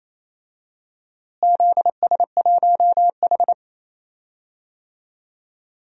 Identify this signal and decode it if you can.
Morse code, heard as a keyed tone, about 28 wpm, 700 Hz. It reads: ZS1H
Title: DXCC-100 Call Signs - 4 Characters - Repeated 3x - 28wpm